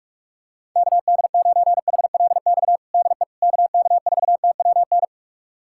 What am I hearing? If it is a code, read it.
Morse code, heard as a keyed tone, about 46 wpm, 690 Hz. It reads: KD9HLX DE KK4TWN
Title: USA Callsign DE Callsign - 6 Characters - 45wpm